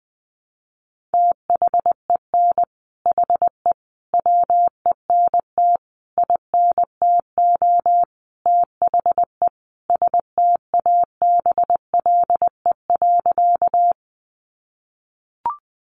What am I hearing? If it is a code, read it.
Morse code, heard as a keyed tone, about 19 wpm, 700 Hz. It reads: THEN HE WENT INTO THE STABLE.
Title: Sentences from Top 1500 Words 20wpm